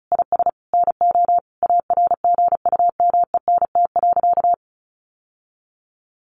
Morse code, heard as a keyed tone, about 34 wpm, 700 Hz. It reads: IS NO ARGUMENT.
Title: Sentences from Top 1400 Words 35wpm